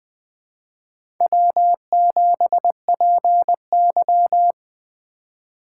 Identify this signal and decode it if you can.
Morse code, heard as a keyed tone, about 20 wpm, 700 Hz. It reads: W7PY
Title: USA Call Signs - 4 Characters - 4x Character Spacing - 20wpm